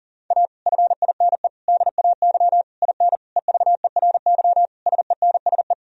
Morse code, heard as a keyed tone, about 40 wpm, 700 Hz. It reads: A FINE DAY IN EVERY SENSE
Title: Sentences from Top 1000 Words - Review Part 4 - 40wpm